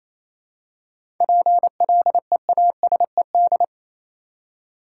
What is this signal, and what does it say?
Morse code, heard as a keyed tone, about 28 wpm, 700 Hz. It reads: PLEASED
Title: Seven Letter Words - 28wpm